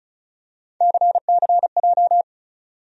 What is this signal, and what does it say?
Morse code, heard as a keyed tone, about 36 wpm, 700 Hz. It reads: CCJ
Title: USA Call Sign Postfixes - 3 Letters 35wpm